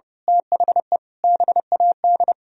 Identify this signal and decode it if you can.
Morse code, heard as a keyed tone, about 32 wpm, 700 Hz. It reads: THE BAD
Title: Sentences from Top 1000 Words - Review Part 3 - 30wpm